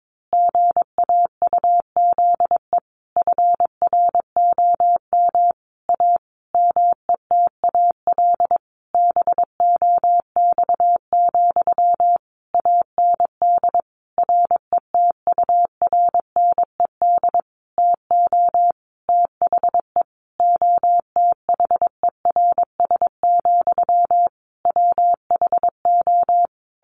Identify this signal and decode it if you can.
Morse code, heard as a keyed tone, about 22 wpm, 700 Hz. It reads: GAUZE FROM A METAL BOX, AND RETURNED TO THE OTHERS, WHO
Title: A Sound Of Thunder 22wpm